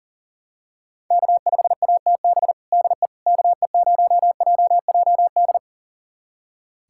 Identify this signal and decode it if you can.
Morse code, heard as a keyed tone, about 40 wpm, 700 Hz. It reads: K5ATB DE KE0JJD
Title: USA Callsign DE Callsign 40wpm